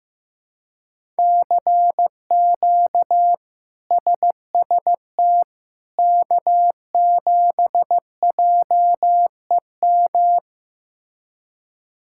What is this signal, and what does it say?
Morse code, heard as a keyed tone, about 15 wpm, 700 Hz. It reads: CQ SST K7JEM
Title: K1USN SST QSO Practice - 15wpm